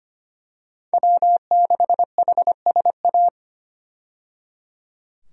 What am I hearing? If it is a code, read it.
Morse code, heard as a keyed tone, about 25 wpm, 700 Hz. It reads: W6HSA
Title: USA Call Signs - 5 Characters - 25wpm